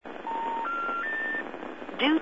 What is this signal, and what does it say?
Special information tone, with speech.